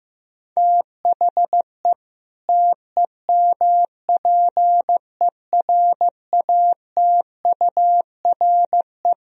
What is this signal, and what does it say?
Morse code, heard as a keyed tone, about 15 wpm, 700 Hz. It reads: THE TEMPERATURE
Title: Sentences from Top 1000 Words - Review Part 4 - 15wpm